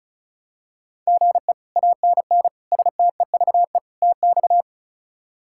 Morse code, heard as a keyed tone, about 36 wpm, 700 Hz. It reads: GE ANN STEVE TX
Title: K1USN SST QSO Practice - 35wpm